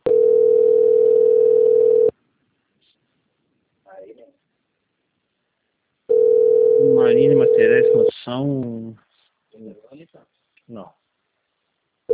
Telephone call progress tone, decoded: ringback tone

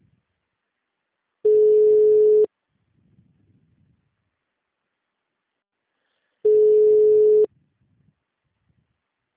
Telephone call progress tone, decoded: ringback tone

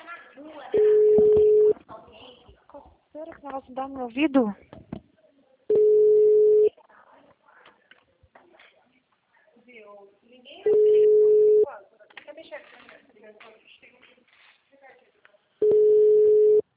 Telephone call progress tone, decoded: ringback tone

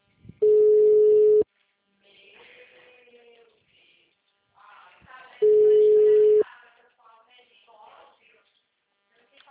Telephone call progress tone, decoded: ringback tone

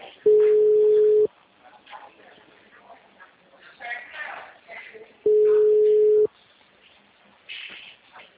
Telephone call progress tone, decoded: ringback tone

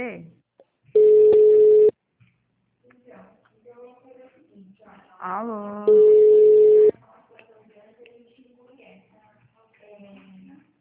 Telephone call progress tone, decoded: ringback tone

